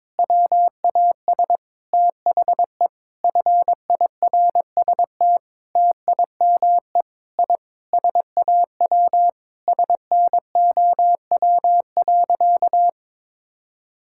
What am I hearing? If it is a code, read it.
Morse code, heard as a keyed tone, about 22 wpm, 700 Hz. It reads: WAS THE FIRST TIME I SAW SNOW.
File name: practice / 22wpm / Sentences with 4-Word Phrases - 22wpm